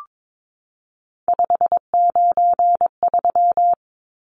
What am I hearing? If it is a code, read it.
Morse code, heard as a keyed tone, about 22 wpm, 700 Hz. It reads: 593
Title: Numbers - 3-Digit 22wpm